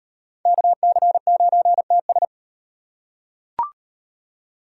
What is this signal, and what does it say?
Morse code, heard as a keyed tone, about 34 wpm, 700 Hz. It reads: KC9TS